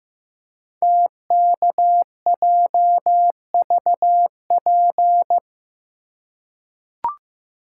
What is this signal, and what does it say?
Morse code, heard as a keyed tone, about 15 wpm, 700 Hz. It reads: TKJVP
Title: Code Groups - 5 Letters - 15wpm